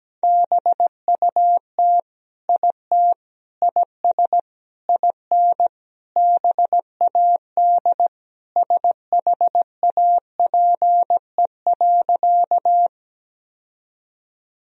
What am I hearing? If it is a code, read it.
Morse code, heard as a keyed tone, about 17 wpm, 700 Hz. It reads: BUT IT IS IN BAD SHAPE.
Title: Sentences from Top 1000 Words - Review Part 3 - 17wpm